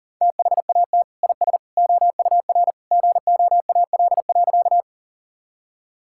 Morse code, heard as a keyed tone, about 40 wpm, 700 Hz. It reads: THAT IS OUR GOAL.
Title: Sentences from Top 1000 Words - Review Part 5 - 40wpm